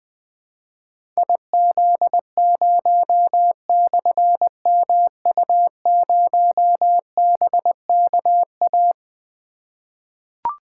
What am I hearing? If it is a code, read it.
Morse code, heard as a keyed tone, about 20 wpm, 700 Hz. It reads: IZ0/MU0BKA
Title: Difficult Call Signs 20wpm